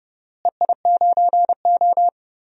Morse code, heard as a keyed tone, about 30 wpm, 700 Hz. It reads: EI9O
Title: DXCC-100 Call Signs - 4 Characters - Repeated 3x - 30wpm